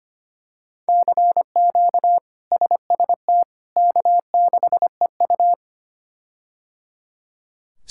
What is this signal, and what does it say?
Morse code, heard as a keyed tone, about 25 wpm, 700 Hz. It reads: CQ SST K6EU
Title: K1USN SST QSO Practice - 25wpm